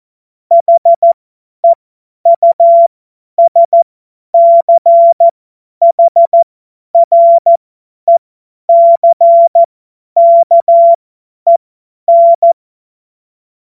Morse code, heard as a keyed tone, about 14 wpm, spaced as for 10 wpm, 680 Hz. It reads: HEUSCHRECKEN
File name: Morsekurs / Witze / Witz 06. Januar 2026 50 F70